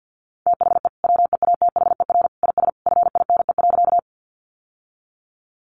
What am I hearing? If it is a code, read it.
Morse code, heard as a keyed tone, about 50 wpm, 700 Hz. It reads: THE WEATHER IS FINE.
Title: Sentences from Top 800 Words 50wpm